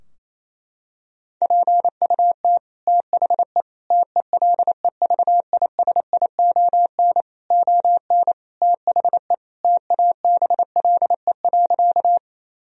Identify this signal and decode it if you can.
Morse code, heard as a keyed tone, about 27 wpm, 700 Hz. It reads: PUT THE TELEVISION ON THE TABLE.
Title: Sentences from Top 1000 Words - Review Part 4 - 28wpm